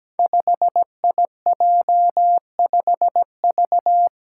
Morse code, heard as a keyed tone, about 17 wpm, 700 Hz. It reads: 5IJ5V
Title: Five Challenging Letter-Number Combinations 17wpm